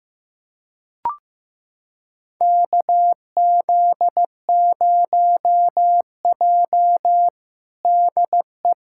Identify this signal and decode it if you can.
Morse code, heard as a keyed tone, about 16 wpm, 700 Hz. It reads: KZ0J DE
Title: USA Callsign DE Callsign - 4 Characters - 15wpm